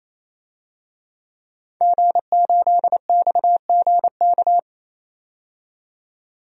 Morse code, heard as a keyed tone, about 28 wpm, 700 Hz. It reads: G8XGK
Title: DXCC-100 Call Signs - 5 Characters - 28wpm